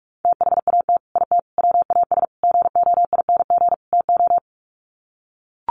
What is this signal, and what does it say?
Morse code, heard as a keyed tone, about 44 wpm, 700 Hz. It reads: THAT IT WAS GOING TO
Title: Sets of 5 Words Encore2 45wpm